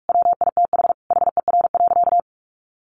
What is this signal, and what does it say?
Morse code, heard as a keyed tone, about 44 wpm, 700 Hz. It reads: WITH HER.
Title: Sentences from Top 100 Words 45wpm